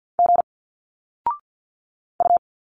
Morse code, heard as a keyed tone, about 50 wpm, 700 Hz. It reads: G U
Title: Single Letters 50wpm